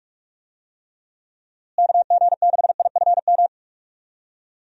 Morse code, heard as a keyed tone, about 46 wpm, 700 Hz. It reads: KG6IFM
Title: USA Call Signs - 6 Characters - Repeated 2x - 45wpm